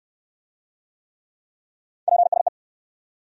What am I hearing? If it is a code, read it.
Morse code, heard as a keyed tone, about 59 wpm, 700 Hz. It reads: PSE